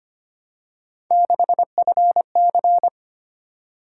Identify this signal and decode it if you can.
Morse code, heard as a keyed tone, about 25 wpm, 700 Hz. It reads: 6FC
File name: Three Letter-Number Combinations - 25wpm